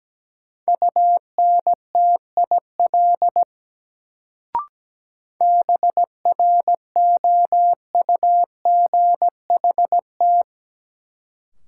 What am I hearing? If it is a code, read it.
Morse code, heard as a keyed tone, about 17 wpm, 700 Hz. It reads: UNTIL BROUGHT
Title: Sentences from Top 500 Words - New Word Drill - 17wpm